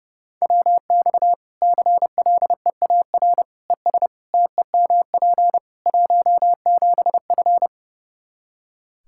Morse code, heard as a keyed tone, about 30 wpm, 700 Hz. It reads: WX CLEAR ES TEMP 17F
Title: Practice QSO - Exchange 2 - Info - 30wpm